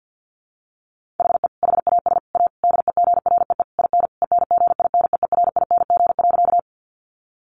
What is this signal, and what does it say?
Morse code, heard as a keyed tone, about 50 wpm, 700 Hz. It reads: HE HAS A DEGREE IN ENGINEERING.
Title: Sentences from Top 1500 Words 50wpm